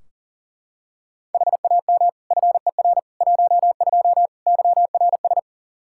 Morse code, heard as a keyed tone, about 40 wpm, 700 Hz. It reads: HAM FER 12 YRS
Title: Practice QSO - Exchange 3 - More Info - 40wpm